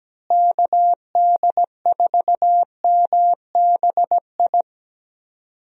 Morse code, heard as a keyed tone, about 17 wpm, 700 Hz. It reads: KD4MBI
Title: USA Call Signs - 6 Characters - Repeated 2x - 17wpm